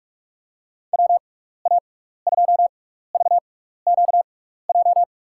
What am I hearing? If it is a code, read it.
Morse code, heard as a keyed tone, about 46 wpm, spaced as for 15 wpm, 700 Hz. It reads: WA2VQJ
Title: USA Call Signs - 6 Characters - 6x Character Spacing - 45wpm